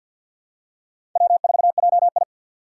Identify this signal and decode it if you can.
Morse code, heard as a keyed tone, about 47 wpm, 700 Hz. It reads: W4JI
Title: USA Call Signs - 4 Characters - 6x Character Spacing - 50wpm